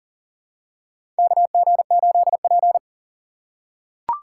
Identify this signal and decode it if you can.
Morse code, heard as a keyed tone, about 40 wpm, 700 Hz. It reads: KG8P
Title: USA Call Signs - 3x Character Spacing - 40wpm